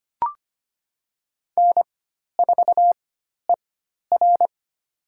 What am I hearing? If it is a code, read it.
Morse code, heard as a keyed tone, about 25 wpm, spaced as for 12 wpm, 690 Hz. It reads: N4ER
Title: USA Call Signs - 4 Characters - 4x Character Spacing - 25wpm